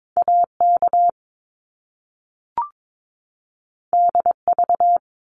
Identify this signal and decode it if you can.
Morse code, heard as a keyed tone, about 22 wpm, 700 Hz. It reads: AK DV
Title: Top Two Letter Combinations in Words 22wpm